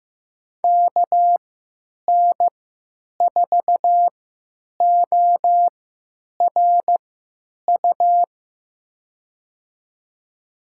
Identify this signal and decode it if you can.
Morse code, heard as a keyed tone, about 15 wpm, spaced as for 9 wpm, 700 Hz. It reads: KN4ORU
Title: USA Call Signs - 6 Characters - 3x Character Spacing - 15wpm